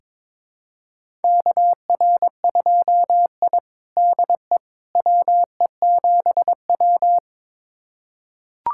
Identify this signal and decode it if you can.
Morse code, heard as a keyed tone, about 22 wpm, 700 Hz. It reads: KR2I DE WE7W
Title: USA Callsign DE Callsign - 4 Characters - 22wpm